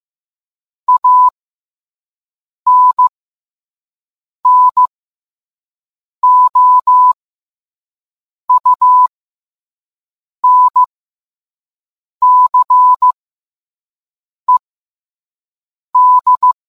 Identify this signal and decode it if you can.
Morse code, heard as a keyed tone, about 15 wpm, spaced as for 5 wpm, 1000 Hz. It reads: ANNOUNCED